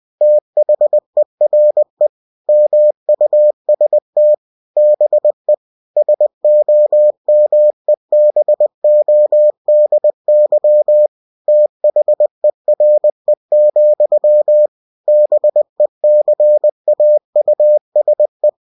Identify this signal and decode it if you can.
Morse code, heard as a keyed tone, about 20 wpm, 600 Hz. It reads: THERE MUST BE SOMEBODY THERE, BECAUSE
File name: cw-books / winnie-the-pooh / winnie-20wpm-page-0014